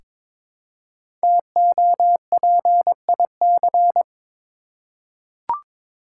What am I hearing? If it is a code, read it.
Morse code, heard as a keyed tone, about 22 wpm, 700 Hz. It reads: TOPIC
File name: 5 Letter Words - First 3 Letters Given - 22wpm